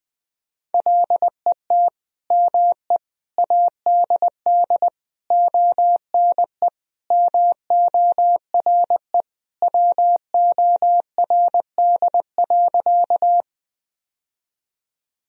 Morse code, heard as a keyed tone, about 20 wpm, 700 Hz. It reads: LET ME ADD ONE MORE WORD.
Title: Sentences from Top 1000 Words - Review Part 1 - 20wpm